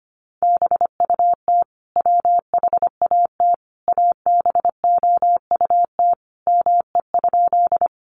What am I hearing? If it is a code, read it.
Morse code, heard as a keyed tone, about 25 wpm, 700 Hz. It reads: BUT WHAT ABOUT ME?
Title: Sentences from Top 200 Words 25wpm